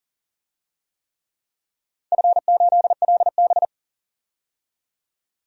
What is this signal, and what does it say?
Morse code, heard as a keyed tone, about 40 wpm, 700 Hz. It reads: F8LB
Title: DXCC-100 Call Signs - 4 Characters - 40wpm